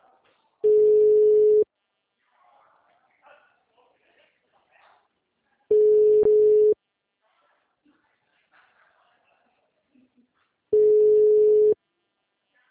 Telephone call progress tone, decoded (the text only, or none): ringback tone